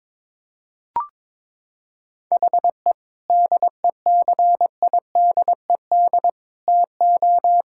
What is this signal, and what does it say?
Morse code, heard as a keyed tone, about 22 wpm, 700 Hz. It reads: HE DECIDED TO
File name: Sentences with 7-Word Phrases - 22wpm